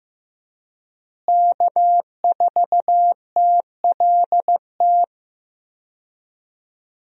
Morse code, heard as a keyed tone, about 15 wpm, 700 Hz. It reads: K4TLT
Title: USA Call Signs - 5 Characters - Repeated 3x - 15wpm